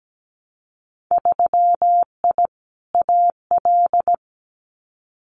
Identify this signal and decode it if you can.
Morse code, heard as a keyed tone, about 17 wpm, 700 Hz. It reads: 3I AL
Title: Winter Field Day Exchanges 17wpm